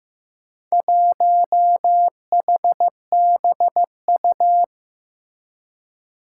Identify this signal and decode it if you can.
Morse code, heard as a keyed tone, about 15 wpm, 700 Hz. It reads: 1HBU
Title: Four Challenging Letter-Number Combinations 15wpm